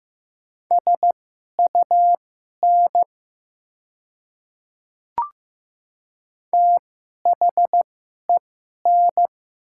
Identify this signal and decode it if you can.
Morse code, heard as a keyed tone, about 15 wpm, spaced as for 11 wpm, 700 Hz. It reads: SUN THEN
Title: 2x Character Spacing - Top 300 Words - 15wpm